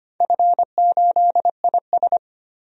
Morse code, heard as a keyed tone, about 25 wpm, 700 Hz. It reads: F8IS
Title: DXCC-100 Call Signs - 4 Characters - 25wpm